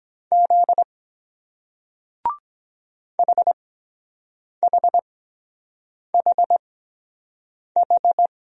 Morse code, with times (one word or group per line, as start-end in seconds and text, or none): 0.3-0.8 Z
3.2-8.3 HHHH